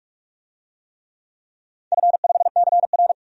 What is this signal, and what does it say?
Morse code, heard as a keyed tone, about 42 wpm, 700 Hz. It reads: F5CR
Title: DXCC-100 Call Signs - 4 Characters - 45wpm